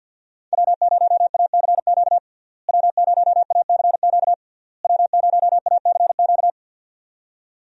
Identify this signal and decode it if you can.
Morse code, heard as a keyed tone, about 50 wpm, 700 Hz. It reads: W0ACQ W0ACQ W0ACQ
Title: USA Call Signs - 5 Characters - Repeated 3x - 50wpm